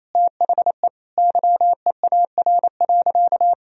Morse code, heard as a keyed tone, about 28 wpm, 700 Hz. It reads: THE YEAR.
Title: Sentences from Top 1000 Words - Review Part 1 - 28wpm